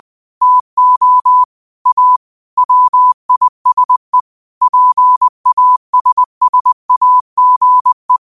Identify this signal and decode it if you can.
Morse code, heard as a keyed tone, about 20 wpm, 1000 Hz. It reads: TO A WISE PASSAGE
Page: New quotes every day in morse code at 20 Words per minute.